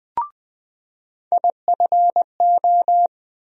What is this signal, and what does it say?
Morse code, heard as a keyed tone, about 20 wpm, 700 Hz. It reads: IFO
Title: USA Call Sign Postfixes - 3 Letters 20wpm